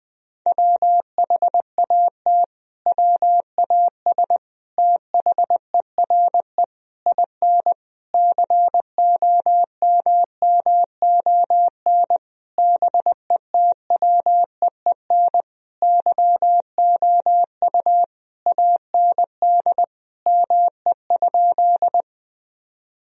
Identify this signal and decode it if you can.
Morse code, heard as a keyed tone, about 20 wpm, 700 Hz. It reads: WHAT WAS THERE IN COMMON BETWEEN YOU AND ME?
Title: Sentences from Top 1000 Words - Review Part 3 - 20wpm